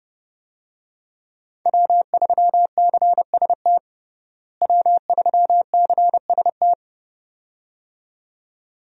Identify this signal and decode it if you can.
Morse code, heard as a keyed tone, about 32 wpm, 700 Hz. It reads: W3CST W3CST
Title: USA Call Signs - 5 Characters - Repeated 2x - 30wpm